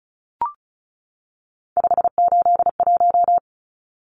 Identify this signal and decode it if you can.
Morse code, heard as a keyed tone, about 34 wpm, 700 Hz. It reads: 581
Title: Numbers - 3-Digit 35wpm